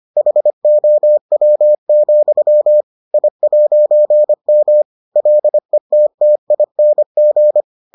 Morse code, heard as a keyed tone, about 25 wpm, 600 Hz. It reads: HOW, I'M LETTING